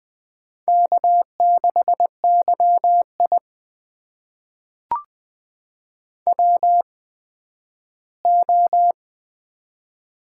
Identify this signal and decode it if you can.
Morse code, heard as a keyed tone, about 20 wpm, spaced as for 12 wpm, 700 Hz. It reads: K6YI WO
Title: USA Call Signs - 4 Characters - 8x Character Spacing - 20wpm